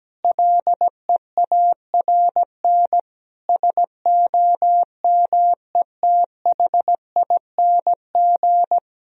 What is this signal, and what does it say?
Morse code, heard as a keyed tone, about 17 wpm, 700 Hz. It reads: LEARN SOMETHING
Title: Sentences from Top 1000 Words - Review Part 4 - 17wpm